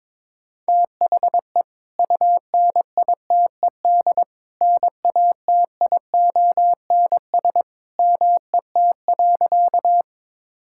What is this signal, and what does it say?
Morse code, heard as a keyed tone, about 22 wpm, 700 Hz. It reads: THE UNITED NATIONS MET.
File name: practice / 22wpm / Sentences with 2-Word Phrases - 22wpm